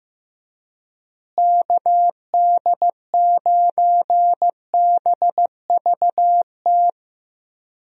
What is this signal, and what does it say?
Morse code, heard as a keyed tone, about 15 wpm, 700 Hz. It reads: KD9BVT
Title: USA Call Signs - 6 Characters - Repeated 2x - 15wpm